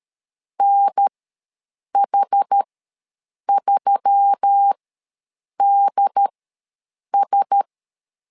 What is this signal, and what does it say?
Morse code, heard as a keyed tone, about 13 wpm, spaced as for 7 wpm, 800 Hz. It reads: NH3DS
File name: LEZIONI / 4 Q1S3 ws 12wpm 2038